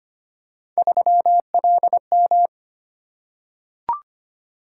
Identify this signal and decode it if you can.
Morse code, heard as a keyed tone, about 25 wpm, 700 Hz. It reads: 3LM
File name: Three Letter-Number Combinations - 25wpm